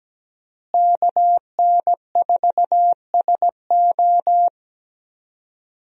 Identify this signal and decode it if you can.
Morse code, heard as a keyed tone, about 17 wpm, 700 Hz. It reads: KN4SO
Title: USA Call Signs - 5 Characters - Repeated 3x - 17wpm